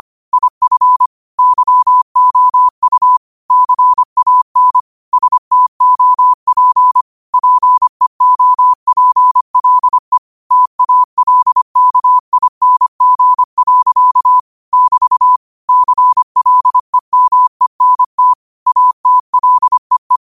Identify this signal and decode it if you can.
Morse code, heard as a keyed tone, about 25 wpm, 1000 Hz. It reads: IF YOU CAN STOP PEOPLE TALKING. <BT> CLEMENT ATLEE
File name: qotdfiles / 25wpm